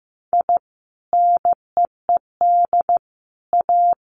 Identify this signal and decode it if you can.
Morse code, heard as a keyed tone, about 15 wpm, 700 Hz. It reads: I NEED A
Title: Sentences from Top 1500 Words 15wpm